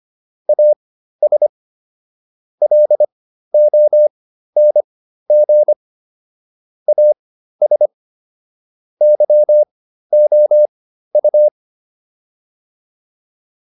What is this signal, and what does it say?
Morse code, heard as a keyed tone, about 25 wpm, spaced as for 13 wpm, 600 Hz. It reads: AS LONG AS YOU
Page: /intermediate-practice-files/